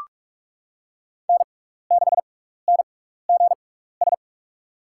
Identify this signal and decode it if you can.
Morse code, heard as a keyed tone, about 43 wpm, spaced as for 15 wpm, 700 Hz. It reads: N6NGS
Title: USA Call Signs - 5 Characters - 6x Character Spacing - 45wpm